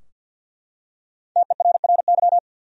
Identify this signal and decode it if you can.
Morse code, heard as a keyed tone, about 52 wpm, 700 Hz. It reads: TERRY
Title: USA Common Names - 50wpm